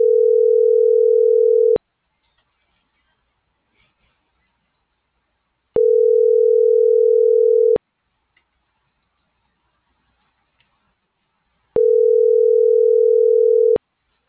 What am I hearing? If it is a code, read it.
Ringback tone (North America), 440 + 480 Hz.
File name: CENTRAL MAERSK / out isabel.bravo